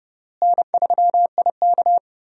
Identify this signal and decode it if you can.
Morse code, heard as a keyed tone, about 30 wpm, 700 Hz. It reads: N3IK
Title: USA Call Signs - 4 Characters - Repeated 3x - 30wpm